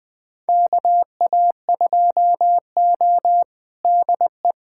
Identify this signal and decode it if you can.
Morse code, heard as a keyed tone, about 20 wpm, 700 Hz. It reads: KA2O DE
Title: USA Callsign DE Callsign - 4 Characters - 20wpm